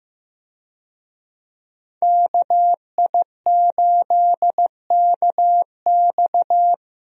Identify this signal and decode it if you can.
Morse code, heard as a keyed tone, about 15 wpm, 700 Hz. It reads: KI8KX
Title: USA Call Signs - 5 Characters - Repeated 3x - 15wpm